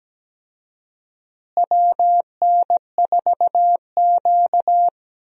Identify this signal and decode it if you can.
Morse code, heard as a keyed tone, about 17 wpm, 700 Hz. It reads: WN4Q
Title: USA Call Signs - 4 Characters - Repeated 3x - 17wpm